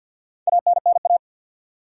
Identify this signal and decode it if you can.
Morse code, heard as a keyed tone, about 51 wpm, 700 Hz. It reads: ANNA
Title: USA Common Names - Max 5 Letters - 50wpm